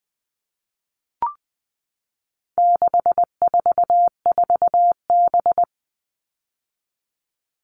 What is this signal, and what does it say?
Morse code, heard as a keyed tone, about 20 wpm, 700 Hz. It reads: 644B
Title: All Cut Numbers - 4-Digit 20wpm